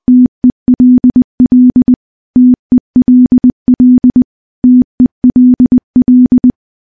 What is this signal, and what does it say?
Morse code, heard as a keyed tone, about 21 wpm, 250 Hz. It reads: TELL TELL TELL
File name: morse-training-20WPM